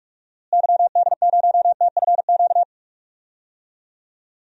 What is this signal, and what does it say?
Morse code, heard as a keyed tone, about 46 wpm, 700 Hz. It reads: YD0TFQ
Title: DXCC-100 Call Signs - 6 Characters - Repeated 3x - 45wpm